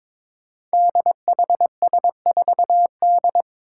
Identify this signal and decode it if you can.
Morse code, heard as a keyed tone, about 22 wpm, 700 Hz. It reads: DHS4D
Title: Five Challenging Letter-Number Combinations 22wpm